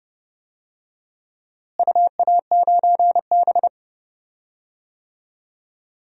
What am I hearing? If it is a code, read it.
Morse code, heard as a keyed tone, about 30 wpm, 700 Hz. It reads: UA9B
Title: DXCC-100 Call Signs - 4 Characters - Repeated 2x - 30wpm